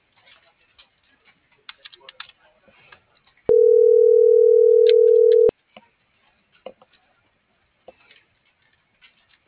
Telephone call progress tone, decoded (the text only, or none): ringback tone